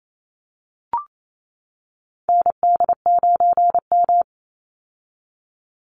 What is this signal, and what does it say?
Morse code, heard as a keyed tone, about 28 wpm, 700 Hz. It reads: ND9M
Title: Call Signs - 28wpm